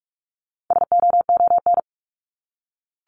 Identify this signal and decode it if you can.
Morse code, heard as a keyed tone, about 44 wpm, 700 Hz. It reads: SOON
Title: Top 500 Words 45wpm